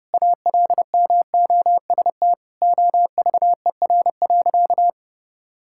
Morse code, heard as a keyed tone, about 30 wpm, 700 Hz. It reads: ALMOST OVER.
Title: Sentences from Top 1600 Words 30wpm